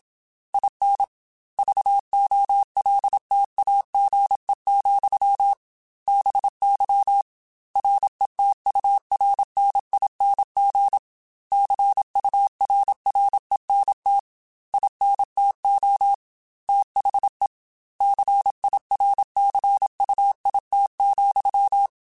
Morse code, heard as a keyed tone, about 27 wpm, 800 Hz. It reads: IN VOLTAGE, BY RETURNING CURRENT INTO THE CIRCUIT,
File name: Technical Subjects / Technical Subjects-mp3s / XC @25